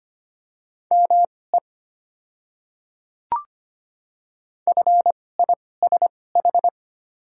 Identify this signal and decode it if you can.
Morse code, heard as a keyed tone, about 25 wpm, spaced as for 18 wpm, 690 Hz. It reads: ME FISH
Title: 2x Character Spacing - Top 300 Words - 25wpm